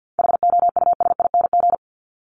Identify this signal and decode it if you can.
Morse code, heard as a keyed tone, about 50 wpm, 700 Hz. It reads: HOUSING
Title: Top 1001-1250 Words 50wpm